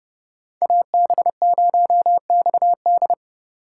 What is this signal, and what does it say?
Morse code, heard as a keyed tone, about 30 wpm, 700 Hz. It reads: AB0XD
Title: USA Call Signs - 8x Character Spacing - 30wpm